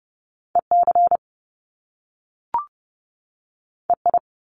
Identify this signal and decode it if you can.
Morse code, heard as a keyed tone, about 30 wpm, 700 Hz. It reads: EC EI
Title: Top Two Letter Combinations in Words 30wpm